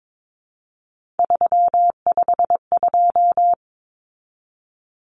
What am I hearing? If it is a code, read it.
Morse code, heard as a keyed tone, about 22 wpm, 700 Hz. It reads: 352
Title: Numbers - 3-Digit 22wpm